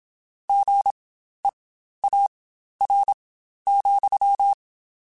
Morse code, heard as a keyed tone, about 27 wpm, spaced as for 12 wpm, 800 Hz. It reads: GEAR,
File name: Spaced / Chuckles Spaced-mp3s / BrainMouth sp @25